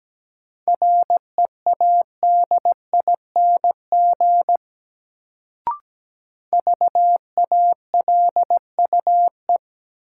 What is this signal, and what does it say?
Morse code, heard as a keyed tone, about 17 wpm, 700 Hz. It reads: READING VALUE
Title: Sentences from Top 600 Words - New Word Drill - 17wpm